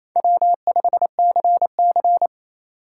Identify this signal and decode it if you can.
Morse code, heard as a keyed tone, about 29 wpm, 700 Hz. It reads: W5CC
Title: USA Call Signs - 4 Characters - 8x Character Spacing - 28wpm